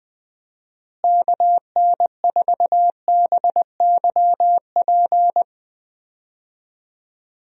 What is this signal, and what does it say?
Morse code, heard as a keyed tone, about 20 wpm, 700 Hz. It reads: KN4BYP
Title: USA Call Signs - 6 Characters - Repeated 2x - 20wpm